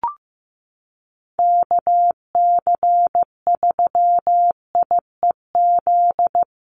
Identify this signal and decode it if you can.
Morse code, heard as a keyed tone, about 15 wpm, 700 Hz. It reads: KC3IEZ
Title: USA Call Signs Repeated 3x 15wpm